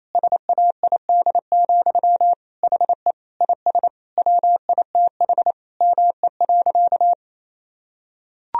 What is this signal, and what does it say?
Morse code, heard as a keyed tone, about 28 wpm, 700 Hz. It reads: SAID, HE IS WITH ME.
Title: Sentences from Top 1000 Words - Review Part 3 - 28wpm